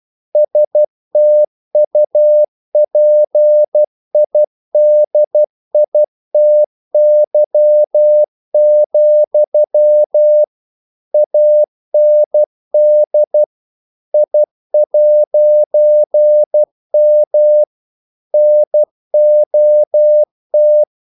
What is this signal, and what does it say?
Morse code, heard as a keyed tone, about 12 wpm, 600 Hz. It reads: STUPIDITY, AND I'M NOT
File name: CW / Practice / EnglishQuotations0112